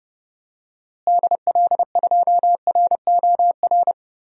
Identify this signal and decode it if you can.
Morse code, heard as a keyed tone, about 30 wpm, 700 Hz. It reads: DL2ROR
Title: DXCC-100 Call Signs - 6 Characters - Repeated 2x - 30wpm